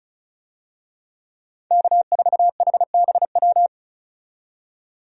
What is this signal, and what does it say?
Morse code, heard as a keyed tone, about 36 wpm, 700 Hz. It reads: K4HBW
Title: USA Call Signs - 5 Characters - 35wpm